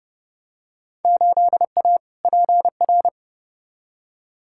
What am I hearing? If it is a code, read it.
Morse code, heard as a keyed tone, about 30 wpm, 700 Hz. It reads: 8A PR
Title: Field Day Exchanges Encore - 30wpm